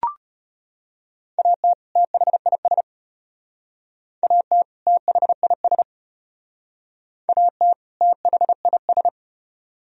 Morse code, heard as a keyed tone, about 31 wpm, 700 Hz. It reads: AT THIS AT THIS AT THIS
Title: Speed-Racing - Sets of 2 Words 25wpm